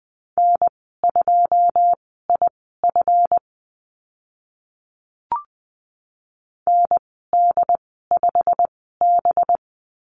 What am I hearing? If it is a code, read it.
Morse code, heard as a keyed tone, about 20 wpm, spaced as for 15 wpm, 700 Hz. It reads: N2IF ND5B
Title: 193 Lesson 31g - Letter v - Call Signs 20wpm